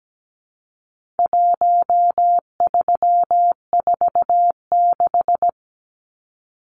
Morse code, heard as a keyed tone, about 17 wpm, 700 Hz. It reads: 1346
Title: Numbers - Contest Style - 4-Digit 17wpm